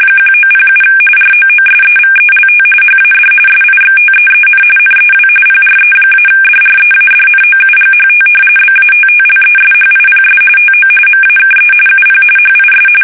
RTTY, about 45.45 baud. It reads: 1907 ROOSEVELT HAD SIXTEEN NEW PRE-DREADNOUGHT BATTLESHIPS TO MAKE UP HIS
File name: fsk850 / usnavy3